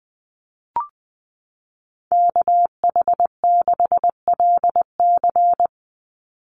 Morse code, heard as a keyed tone, about 19 wpm, 700 Hz. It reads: KH6LC
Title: Call Signs Repeated 3x - 20wpm